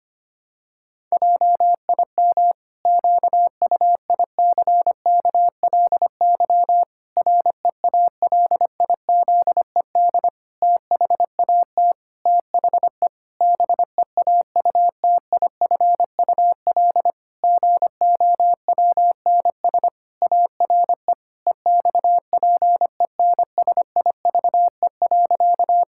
Morse code, heard as a keyed tone, about 25 wpm, 700 Hz. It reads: JIM QUICKLY REALIZED THAT THE BEAUTIFUL GOWNS ARE EXPENSIVE.
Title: Pangram Challenge v2 - 25wpm